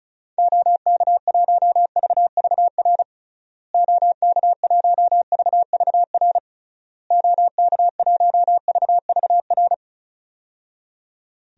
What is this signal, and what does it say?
Morse code, heard as a keyed tone, about 34 wpm, 700 Hz. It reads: OK1VVR OK1VVR OK1VVR
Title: DXCC-100 Call Signs - 6 Characters - Repeated 3x - 35wpm